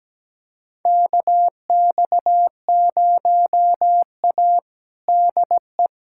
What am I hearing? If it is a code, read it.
Morse code, heard as a keyed tone, about 17 wpm, 700 Hz. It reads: KX0A DE
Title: USA Callsign DE Callsign 17wpm